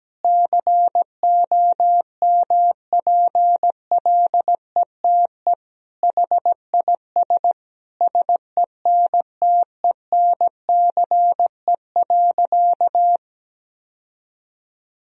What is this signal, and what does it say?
Morse code, heard as a keyed tone, about 17 wpm, 700 Hz. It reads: COMPLETE HIS SENTENCE.
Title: Sentences from Top 1000 Words - Review Part 1 - 17wpm